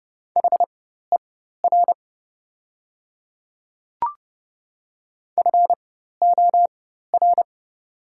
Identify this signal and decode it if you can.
Morse code, heard as a keyed tone, about 29 wpm, spaced as for 14 wpm, 700 Hz. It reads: HER FOR
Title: 4x Character Spacing - Top 100 Words - 30wpm